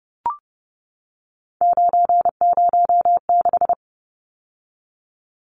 Morse code, heard as a keyed tone, about 30 wpm, 700 Hz. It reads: 906
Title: Numbers - 3-Digit 30wpm